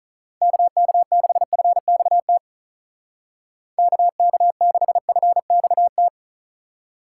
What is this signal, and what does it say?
Morse code, heard as a keyed tone, about 45 wpm, 700 Hz. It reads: KK6FXT KK6FXT